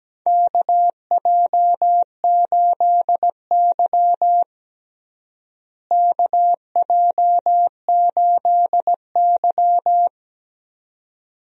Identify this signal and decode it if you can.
Morse code, heard as a keyed tone, about 17 wpm, 700 Hz. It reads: KJ8Y KJ8Y
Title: USA Call Signs - 4 Characters - Repeated 2x - 17wpm